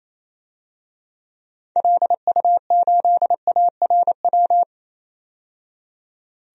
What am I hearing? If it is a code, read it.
Morse code, heard as a keyed tone, about 28 wpm, 700 Hz. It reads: LU8ARW
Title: DXCC-100 Call Signs - 6 Characters - 28wpm